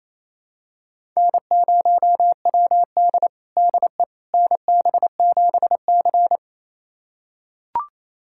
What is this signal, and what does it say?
Morse code, heard as a keyed tone, about 28 wpm, 700 Hz. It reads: N0WD DE NB7C
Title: USA Callsign DE Callsign - 4 Characters - 28wpm